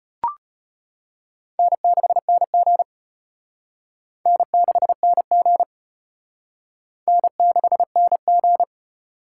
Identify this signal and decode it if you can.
Morse code, heard as a keyed tone, about 32 wpm, 700 Hz. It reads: N6NG N6NG N6NG